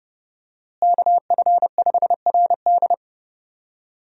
Morse code, heard as a keyed tone, about 30 wpm, 700 Hz. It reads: KF5RD
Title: USA Call Signs - 5 Characters - Repeated 2x - 30wpm